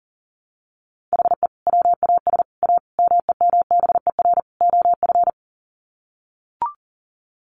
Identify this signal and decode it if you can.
Morse code, heard as a keyed tone, about 40 wpm, 700 Hz. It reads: HE WAS A MEMBER OF
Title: Sets of 5 Words Encore2 40wpm